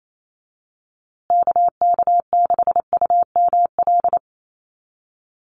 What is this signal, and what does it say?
Morse code, heard as a keyed tone, about 28 wpm, 700 Hz. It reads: KK6UML
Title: USA Call Signs Repeated 3x 28wpm